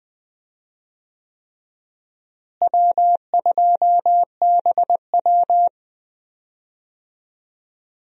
Morse code, heard as a keyed tone, about 21 wpm, 700 Hz. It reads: W2BW
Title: USA Call Signs - 4 Characters - Repeated 3x - 20wpm